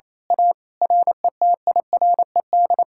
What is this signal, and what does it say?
Morse code, heard as a keyed tone, about 26 wpm, 700 Hz. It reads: A RETIRED
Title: Speed-Racing Common QSO Phrases - 28wpm